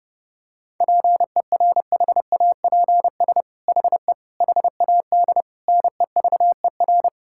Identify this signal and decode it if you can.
Morse code, heard as a keyed tone, about 30 wpm, 700 Hz. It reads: PERHAPS HE HAD NEVER
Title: Sentences from Top 1600 Words 30wpm